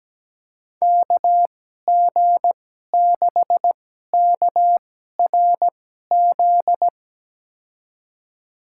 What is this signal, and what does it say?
Morse code, heard as a keyed tone, about 17 wpm, spaced as for 12 wpm, 700 Hz. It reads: KG6KRZ
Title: USA Call Signs - 6 Characters - 2x Character Spacing - 17wpm